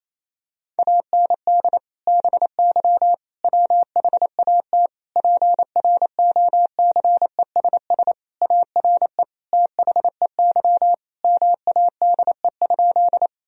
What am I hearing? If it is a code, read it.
Morse code, heard as a keyed tone, about 28 wpm, 700 Hz. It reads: AND BY WHAT PROCESS ARE THEY MADE?
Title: Sentences from Top 1000 Words - Review Part 3 - 28wpm